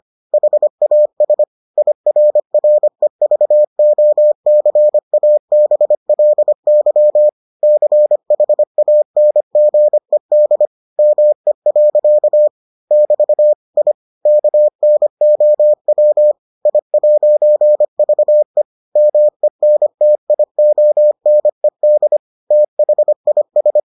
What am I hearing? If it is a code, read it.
Morse code, heard as a keyed tone, about 25 wpm, 600 Hz. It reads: HAS IRREVOCABLY CHANGED ME. <BT> I KNOW I'VE MENTIONED THIS